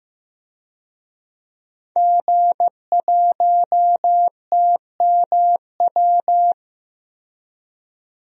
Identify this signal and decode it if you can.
Morse code, heard as a keyed tone, about 15 wpm, 700 Hz. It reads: G1TMW
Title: DXCC-100 Call Signs - 5 Characters - 15wpm